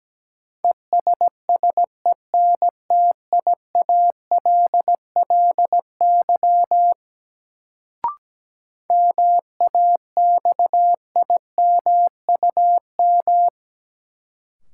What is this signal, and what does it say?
Morse code, heard as a keyed tone, about 17 wpm, 700 Hz. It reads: ESSENTIALLY MAXIMUM
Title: Sentences from Top 1300 Words - New Word Drill Alt - 17wpm